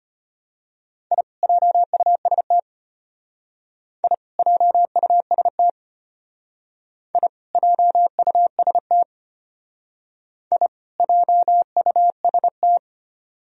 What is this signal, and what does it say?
Morse code, heard as a keyed tone, about 32 wpm, 700 Hz. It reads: I JUST I JUST I JUST I JUST